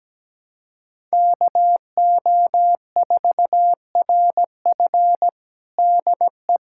Morse code, heard as a keyed tone, about 17 wpm, 700 Hz. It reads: KO4RF DE
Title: USA Callsign DE Callsign - 5 Characters - 17wpm